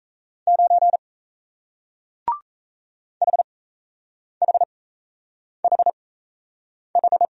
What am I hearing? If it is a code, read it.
Morse code, heard as a keyed tone, about 38 wpm, spaced as for 8 wpm, 700 Hz. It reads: 9 HHHH